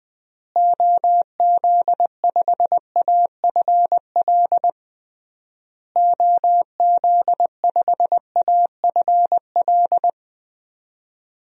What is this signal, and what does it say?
Morse code, heard as a keyed tone, about 20 wpm, 700 Hz. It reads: OZ5AFL OZ5AFL
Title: DXCC-100 Call Signs - 6 Characters - Repeated 3x - 20wpm